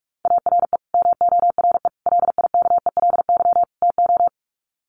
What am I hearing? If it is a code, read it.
Morse code, heard as a keyed tone, about 44 wpm, 700 Hz. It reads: ARE MORE LIKELY TO
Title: Sets of 4 Words 45wpm